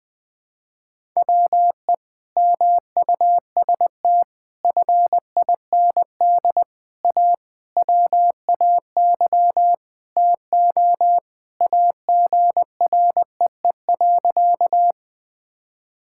Morse code, heard as a keyed tone, about 20 wpm, 700 Hz. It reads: WE MUST FIND A WAY TO AGREE.
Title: Sentences with 5-Word Phrases - 20wpm